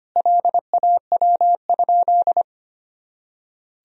Morse code, heard as a keyed tone, about 25 wpm, 700 Hz. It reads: LAW?
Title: Sentences from Top 1000 Words - Review Part 4 - 25wpm